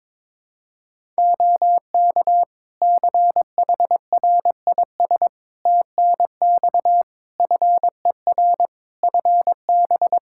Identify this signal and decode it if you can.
Morse code, heard as a keyed tone, about 22 wpm, 700 Hz. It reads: OK CHRIS TNX FER FB
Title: Practice QSO - Exchange 4 - Ending - 22wpm